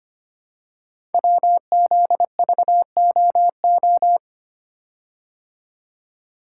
Morse code, heard as a keyed tone, about 25 wpm, 700 Hz. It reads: WZVOO
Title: Code Groups - 5 Letter:Numbers - 25wpm